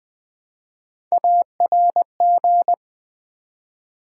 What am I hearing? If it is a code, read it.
Morse code, heard as a keyed tone, about 20 wpm, 700 Hz. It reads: ARG
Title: Build A Word - 1 Letter At A Time - 3 to 5-Letter Words - 20wpm